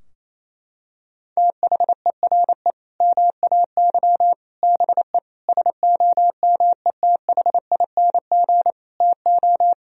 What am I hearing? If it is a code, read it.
Morse code, heard as a keyed tone, about 28 wpm, 700 Hz. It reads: THERE MAY BE SOMETHING TO
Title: Sentences from Top 1000 Words - Review Part 3 - 28wpm